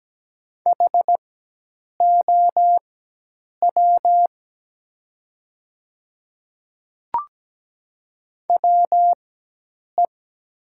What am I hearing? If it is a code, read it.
Morse code, heard as a keyed tone, about 17 wpm, spaced as for 8 wpm, 700 Hz. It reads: HOW WE
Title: 4x Character Spacing - Top 300 Words - 17wpm